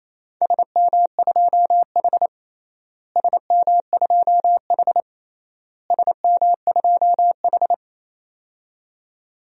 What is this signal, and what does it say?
Morse code, heard as a keyed tone, about 28 wpm, 700 Hz. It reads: SM2H SM2H SM2H
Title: DXCC-100 Call Signs - 4 Characters - Repeated 3x - 28wpm